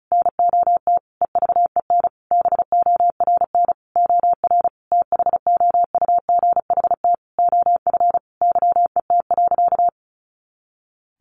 Morse code, heard as a keyed tone, about 34 wpm, 700 Hz. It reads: NOT EVEN BORN OR THOUGHT OF YET.
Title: A Sound Of Thunder 35wpm